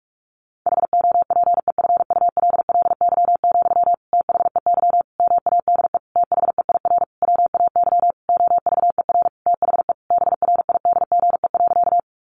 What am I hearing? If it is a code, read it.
Morse code, heard as a keyed tone, about 44 wpm, 700 Hz. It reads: HOPEFULLY, THEY MADE THEIR WAY OVER THE BRIDGE.
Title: Sentences from Top 1500 Words 45wpm